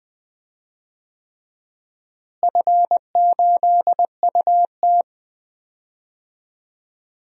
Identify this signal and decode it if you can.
Morse code, heard as a keyed tone, about 21 wpm, 700 Hz. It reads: F8UT
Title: DXCC-100 Call Signs - 4 Characters - Repeated 3x - 20wpm